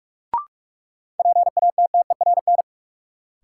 Morse code, heard as a keyed tone, about 46 wpm, 700 Hz. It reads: PATTERN
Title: Sentences from Top 1000 Words - New Word Drill - 45wpm